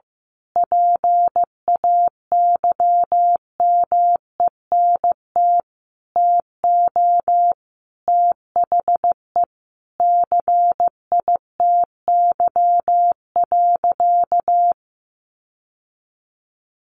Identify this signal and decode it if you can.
Morse code, heard as a keyed tone, about 15 wpm, 700 Hz. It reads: PAYMENT TO THE CITY.
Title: Sentences from Top 1400 Words 15wpm